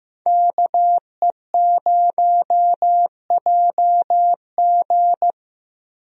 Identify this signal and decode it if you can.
Morse code, heard as a keyed tone, about 16 wpm, 700 Hz. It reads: KE0JG
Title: USA Call Signs - 5 Characters - Repeated 3x - 15wpm